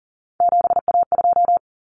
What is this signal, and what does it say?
Morse code, heard as a keyed tone, about 40 wpm, 700 Hz. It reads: 7A2
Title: Common Cut Numbers - 3-Digit 40wpm